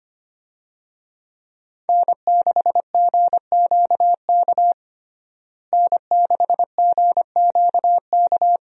Morse code, heard as a keyed tone, about 25 wpm, 700 Hz. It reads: N6GQK N6GQK
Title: USA Call Signs - 5 Characters - Repeated 2x - 25wpm